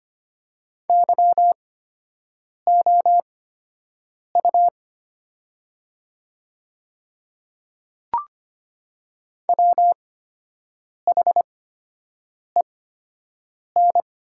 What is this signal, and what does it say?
Morse code, heard as a keyed tone, about 25 wpm, spaced as for 7 wpm, 700 Hz. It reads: YOU WHEN
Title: 8x Character Spacing - Top 200 Words - 25wpm